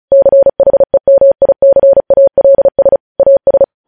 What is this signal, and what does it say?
Morse code, heard as a keyed tone, about 36 wpm, 550 Hz. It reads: CHEMICALS AS
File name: podcast / News headlines-1x-35x35